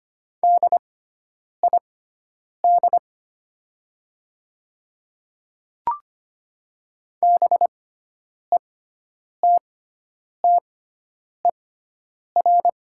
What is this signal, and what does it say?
Morse code, heard as a keyed tone, about 25 wpm, spaced as for 9 wpm, 700 Hz. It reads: DID BETTER
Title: 6x Character Spacing - Top 500 Words - 25wpm